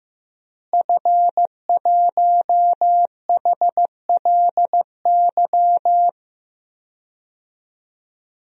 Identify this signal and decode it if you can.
Morse code, heard as a keyed tone, about 15 wpm, 700 Hz. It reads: F1HLY
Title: DXCC-100 Call Signs - 5 Characters - Repeated 3x - 15wpm